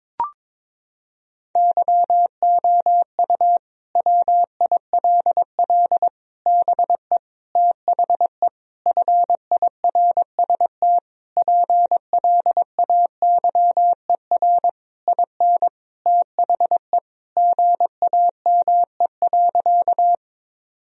Morse code, heard as a keyed tone, about 22 wpm, 700 Hz. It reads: YOU WILL BE THE FIRST PLAYER IN THE GAME.
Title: Sentences from Top 1000 Words - Review Part 4 - 22wpm